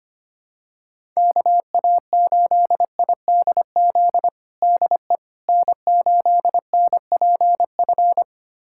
Morse code, heard as a keyed tone, about 25 wpm, 700 Hz. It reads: KA8IDZ DE N8NPF
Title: USA Callsign DE Callsign 25wpm